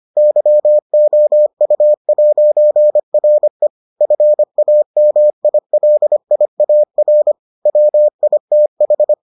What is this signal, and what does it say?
Morse code, heard as a keyed tone, about 25 wpm, 600 Hz. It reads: YOU'RE FAMILIAR WITH